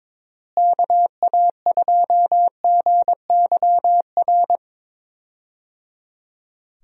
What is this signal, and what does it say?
Morse code, heard as a keyed tone, about 22 wpm, 700 Hz. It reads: KA2GYR
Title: USA Call Signs - 6 Characters - Repeated 3x - 22wpm